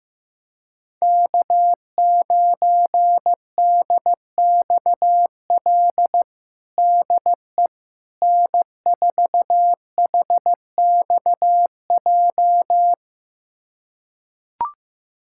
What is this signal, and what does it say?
Morse code, heard as a keyed tone, about 15 wpm, 700 Hz. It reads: K9DXL DE N4HXJ
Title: USA Callsign DE Callsign - 5 Characters - 15wpm